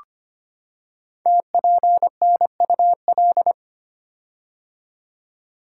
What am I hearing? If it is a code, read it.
Morse code, heard as a keyed tone, about 25 wpm, 700 Hz. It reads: TPNUL
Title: Code Groups - 5 Letters - 25wpm